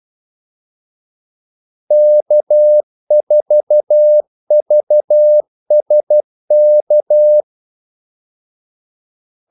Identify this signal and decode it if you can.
Morse code, heard as a keyed tone, about 12 wpm, 600 Hz. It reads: K4VSK
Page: 2 ekstra ordmellemrum mellem kaldesignalerne